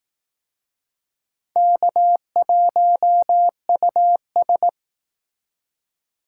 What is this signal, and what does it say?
Morse code, heard as a keyed tone, about 19 wpm, 700 Hz. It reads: K1US
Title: Speed Racing - USA 1x2 Call Signs 15wpm